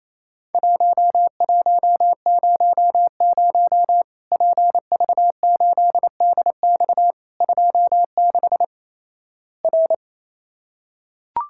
Morse code, heard as a keyed tone, about 28 wpm, 700 Hz. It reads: 1100 PV8DX 26 R
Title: Worked All Europe DX Contest - QTC Exchange - 28wpm